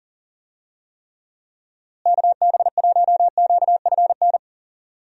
Morse code, heard as a keyed tone, about 40 wpm, 700 Hz. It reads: KB1QFN
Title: USA Call Signs - 6 Characters - Repeated 2x - 40wpm